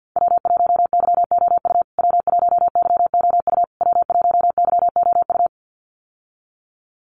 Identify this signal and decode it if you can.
Morse code, heard as a keyed tone, about 50 wpm, 700 Hz. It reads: W1YOU W1YOU W1YOU
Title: USA Call Signs Repeated 3x 50wpm